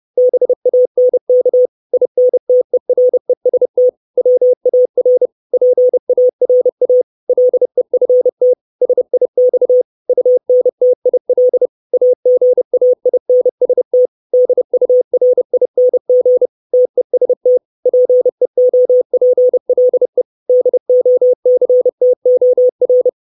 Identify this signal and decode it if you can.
Morse code, heard as a keyed tone, about 30 wpm, 500 Hz. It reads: BANK INTEREST WAR PARA LEFT SIX UNTIL AGAINST DURING TEST PEOPLE DOCTOR
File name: 632 Words 30WPM